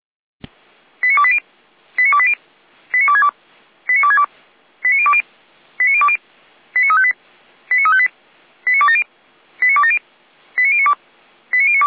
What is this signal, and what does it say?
Five-tone selective call (ZVEI), heard as quick runs of five tones, tones of about 70 ms. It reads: 89280 89280 89262 89262 89020 89020 89358 89358 89270 89270 89092 89092
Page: SDR/ZVEI